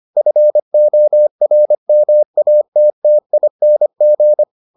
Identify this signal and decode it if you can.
Morse code, heard as a keyed tone, about 25 wpm, 600 Hz. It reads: FORMATTING